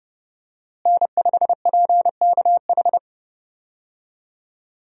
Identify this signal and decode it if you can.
Morse code, heard as a keyed tone, about 30 wpm, 700 Hz. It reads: N5PKH
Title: USA Call Signs - 5 Characters - Repeated 2x - 30wpm